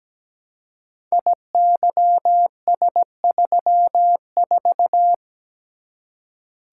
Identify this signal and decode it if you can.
Morse code, heard as a keyed tone, about 17 wpm, 700 Hz. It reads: IYS34
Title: Code Groups - 5 Letter:Numbers - 17wpm